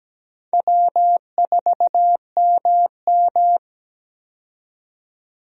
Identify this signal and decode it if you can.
Morse code, heard as a keyed tone, about 17 wpm, 700 Hz. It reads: W4MM
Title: USA Call Signs - 4 Characters - 17wpm